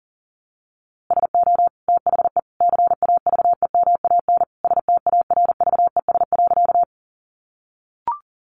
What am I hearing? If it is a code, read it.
Morse code, heard as a keyed tone, about 40 wpm, 700 Hz. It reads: SO THE CAVEMAN STARVES.
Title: A Sound Of Thunder 40wpm